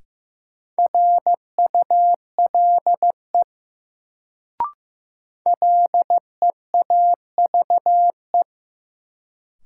Morse code, heard as a keyed tone, about 15 wpm, 700 Hz. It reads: RULE LEAVE
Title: Sentences from Top 400 Words - New Word Drill - 15wpm